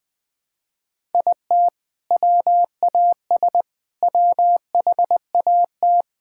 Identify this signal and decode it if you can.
Morse code, heard as a keyed tone, about 21 wpm, 700 Hz. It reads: IT WAS WHAT
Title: Sentences with 6-Word Phrases - 20wpm